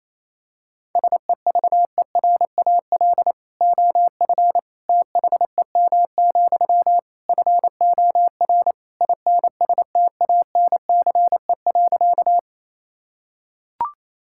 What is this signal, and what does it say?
Morse code, heard as a keyed tone, about 28 wpm, 700 Hz. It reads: SEVERAL OF THEM, FOR INSTANCE.
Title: Sentences from Top 1000 Words - Review Part 5 - 28wpm